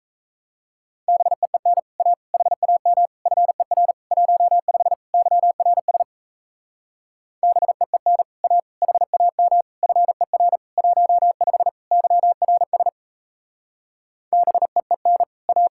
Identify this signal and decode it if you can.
Morse code, heard as a keyed tone, about 36 wpm, 700 Hz. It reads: BEEN A HAM FER 15 YRS BEEN A HAM FER 15 YRS BEEN A